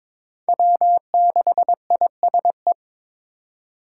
Morse code, heard as a keyed tone, about 21 wpm, 700 Hz. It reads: W6ISE
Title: USA Call Signs - 6x Character Spacing - 22wpm